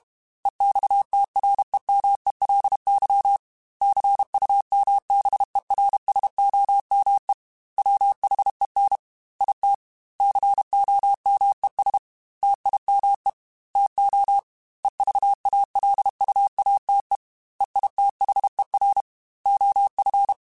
Morse code, heard as a keyed tone, about 32 wpm, 800 Hz. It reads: EXTREMELY CUMBERSOME WHEN IT COMES TIME TO EVALUATE EITHER OF
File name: Technical Subjects / Technical Subjects-mp3s / AND Gate @30